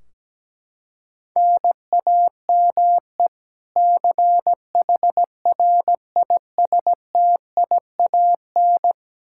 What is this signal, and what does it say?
Morse code, heard as a keyed tone, about 17 wpm, 700 Hz. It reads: NAME CHRISTIAN
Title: K1USN SST QSO Practice 17wpm